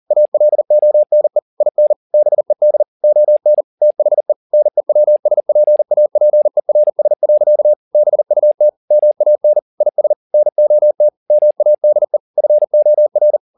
Morse code, heard as a keyed tone, about 40 wpm, 600 Hz. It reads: ALONE IN BED ON THE NEWSPAPERS. BUT MAN IS NOT MADE FOR
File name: the old man and the sea 2 40wpm